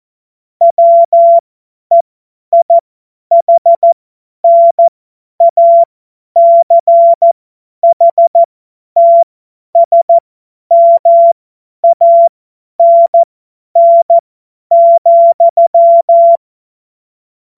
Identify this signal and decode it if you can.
Morse code, heard as a keyed tone, about 14 wpm, spaced as for 10 wpm, 680 Hz. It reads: WEIHNACHTSMANN,